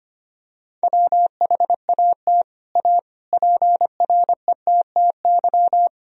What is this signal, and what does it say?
Morse code, heard as a keyed tone, about 25 wpm, 700 Hz. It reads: WHAT A PRETTY
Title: Sentences from Top 1000 Words - Review Part 1 - 25wpm